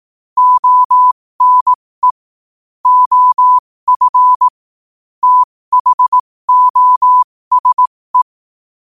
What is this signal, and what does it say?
Morse code, heard as a keyed tone, about 18 wpm, spaced as for 15 wpm, 1000 Hz. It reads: ONE OF THOSE